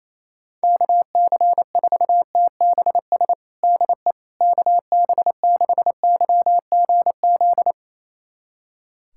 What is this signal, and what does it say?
Morse code, heard as a keyed tone, about 28 wpm, 700 Hz. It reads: KC4TBS DE KB6YGZ
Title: USA Callsign DE Callsign - 6 Characters - 28wpm